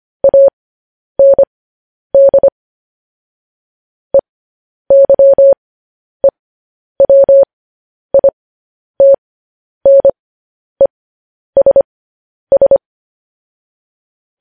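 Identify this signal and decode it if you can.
Morse code, heard as a keyed tone, about 25 wpm, spaced as for 10 wpm, 550 Hz. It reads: AND EYEWITNESS
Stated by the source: Morse code news headlines practice at 10 wpm.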